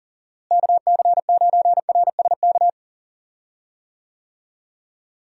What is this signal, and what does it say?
Morse code, heard as a keyed tone, about 38 wpm, 700 Hz. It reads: KC9RSK
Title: USA Call Signs - 6 Characters - Repeated 2x - 40wpm